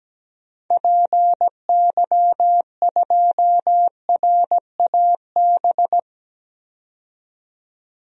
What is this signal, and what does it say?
Morse code, heard as a keyed tone, about 17 wpm, 700 Hz. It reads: PY2RAB
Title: DXCC-100 Call Signs - 6 Characters - 17wpm